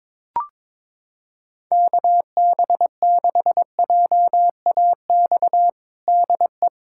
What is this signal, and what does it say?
Morse code, heard as a keyed tone, about 22 wpm, 700 Hz. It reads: KB6JAX DE
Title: USA Callsign DE Callsign - 6 Characters - 22wpm